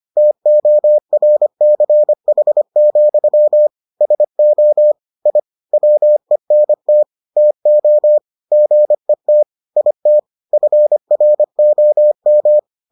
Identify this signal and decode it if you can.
Morse code, heard as a keyed tone, about 25 wpm, 600 Hz. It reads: TORCH, SO I WENT TO GET IT FROM